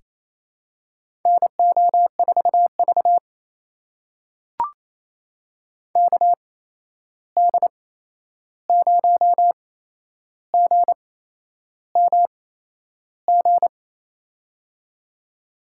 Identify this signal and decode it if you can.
Morse code, heard as a keyed tone, about 28 wpm, spaced as for 11 wpm, 700 Hz. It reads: NO4V KD0GMG
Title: USA Call Signs - 8x Character Spacing - 28wpm